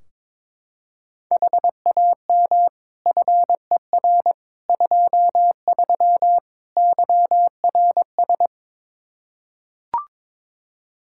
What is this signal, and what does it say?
Morse code, heard as a keyed tone, about 22 wpm, 700 Hz. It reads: HAM FER 23 YRS
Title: Practice QSO - Exchange 3 - More Info - 22wpm